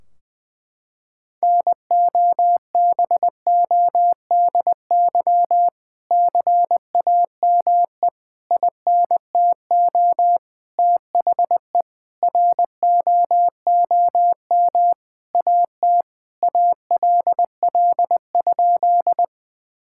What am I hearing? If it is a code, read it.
Morse code, heard as a keyed tone, about 21 wpm, 700 Hz. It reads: NOBODY CAME INTO THE ROOM AT ALL?
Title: Sentences from Top 1600 Words 20wpm